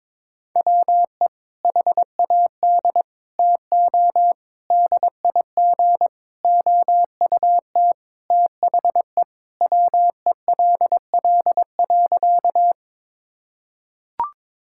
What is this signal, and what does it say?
Morse code, heard as a keyed tone, about 22 wpm, 700 Hz. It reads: WE HAD TO DIG OUT THE WELL.
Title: Sentences from Top 1600 Words 22wpm